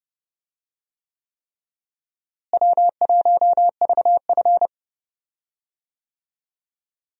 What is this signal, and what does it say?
Morse code, heard as a keyed tone, about 30 wpm, 700 Hz. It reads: W1VF
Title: USA Call Signs - 4 Characters - Repeated 3x - 30wpm